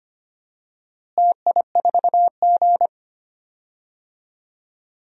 Morse code, heard as a keyed tone, about 25 wpm, 700 Hz. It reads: TI4G
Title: DXCC-100 Call Signs - 4 Characters - Repeated 3x - 25wpm